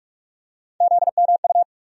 Morse code, heard as a keyed tone, about 47 wpm, 700 Hz. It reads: ZMU
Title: USA Call Sign Postfixes - 3 Letters 45wpm